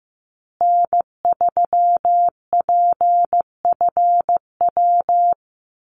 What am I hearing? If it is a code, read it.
Morse code, heard as a keyed tone, about 15 wpm, 700 Hz. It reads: N3PFW
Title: USA Call Signs Repeated 3x 15wpm